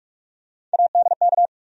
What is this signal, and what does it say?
Morse code, heard as a keyed tone, about 47 wpm, 700 Hz. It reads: ADK
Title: USA Call Sign Postfixes - 3 Letters 45wpm